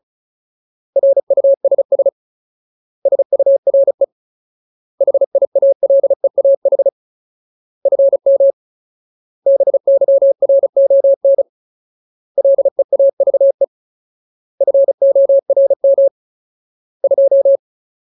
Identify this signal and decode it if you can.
Morse code, heard as a keyed tone, about 34 wpm, 550 Hz. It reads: RUSS SURE HIALEAH FM BYRON LEAVE FORM 2
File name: Ultimate Word List / 35WPM Ultimate